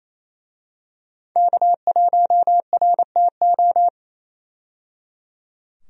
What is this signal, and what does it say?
Morse code, heard as a keyed tone, about 27 wpm, 700 Hz. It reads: K1RTO
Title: USA Call Signs - 5 Characters - 28wpm